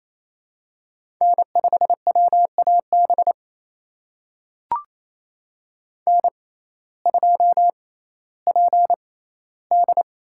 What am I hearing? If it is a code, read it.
Morse code, heard as a keyed tone, about 27 wpm, spaced as for 15 wpm, 700 Hz. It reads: N5WAB N2PD
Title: USA Call Signs - 5 Characters - 6x Character Spacing - 28wpm